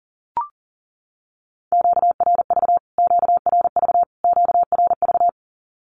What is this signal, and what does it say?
Morse code, heard as a keyed tone, about 37 wpm, 700 Hz. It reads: QRV QRV QRV
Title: Q-Codes Repeated 3x 40wpm